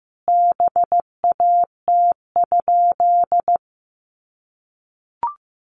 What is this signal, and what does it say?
Morse code, heard as a keyed tone, about 15 wpm, 700 Hz. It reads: BAT?
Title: Sentences from Top 1500 Words 15wpm